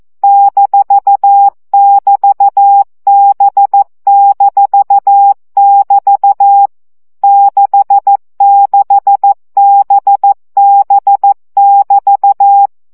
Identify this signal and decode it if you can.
Morse code, heard as a keyed tone, about 14 wpm, 800 Hz. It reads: -<BT>B-<BT> 66BB<BT>
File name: CW gyakorlas b 6 egyenlo minusz 12 baud